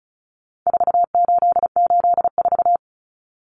Morse code, heard as a keyed tone, about 34 wpm, 700 Hz. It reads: 4884
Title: Common Cut Numbers - 4-Digit 35wpm